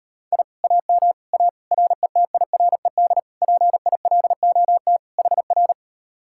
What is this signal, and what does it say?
Morse code, heard as a keyed tone, about 38 wpm, 700 Hz. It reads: I AM A RETIRED PILOT HR